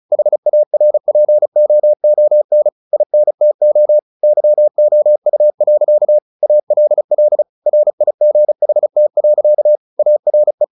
Morse code, heard as a keyed tone, about 35 wpm, 600 Hz. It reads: HARPOON INTO YOU. ALL RIGHT. ARE
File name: the old man and the sea / the old man and the sea 1 35wpm